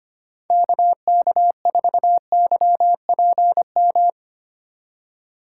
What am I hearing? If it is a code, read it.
Morse code, heard as a keyed tone, about 25 wpm, 700 Hz. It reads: KK4YPM
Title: USA Call Signs - 6 Characters - 2x Character Spacing - 25wpm